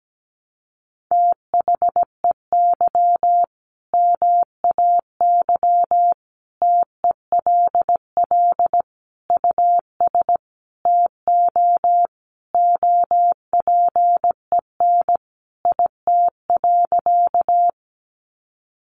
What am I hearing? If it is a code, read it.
Morse code, heard as a keyed tone, about 17 wpm, 700 Hz. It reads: THEY MAY TELL US TO OPEN IT.
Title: Sentences from Top 300 Words 17wpm